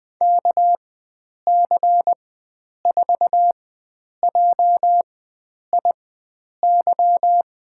Morse code, heard as a keyed tone, about 20 wpm, spaced as for 9 wpm, 700 Hz. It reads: KC4JIY
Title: USA Call Signs - 4x Character Spacing - 20wpm